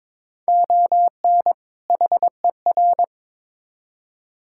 Morse code, heard as a keyed tone, about 22 wpm, 700 Hz. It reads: ON HER
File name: Build A Phrase - 2-Word Phrases - 22wpm